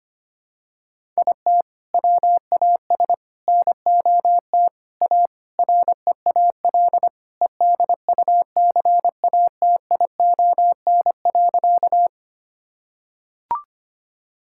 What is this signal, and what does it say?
Morse code, heard as a keyed tone, about 25 wpm, 700 Hz. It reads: IT WAS NOT A REAL EDUCATION.
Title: Sentences from Top 1000 Words - Review Part 6 - 25wpm